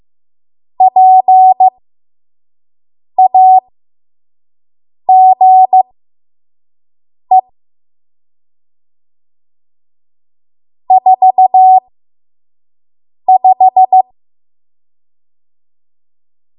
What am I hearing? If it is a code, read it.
Morse code, heard as a keyed tone, about 15 wpm, spaced as for 5 wpm, 750 Hz. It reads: PAGE 45
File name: Morse / 05WPM